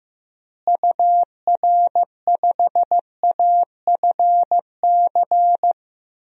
Morse code, heard as a keyed tone, about 16 wpm, 700 Hz. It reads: UR5AFC
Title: DXCC-100 Call Signs - 6 Characters - Repeated 2x - 15wpm